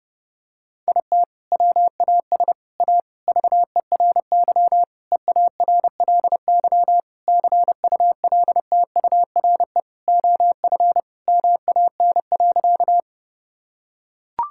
Morse code, heard as a keyed tone, about 30 wpm, 700 Hz. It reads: IT WAS A VERY EARLY CULTURE OF MAN.
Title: Sentences from Top 1000 Words - Review Part 6 - 30wpm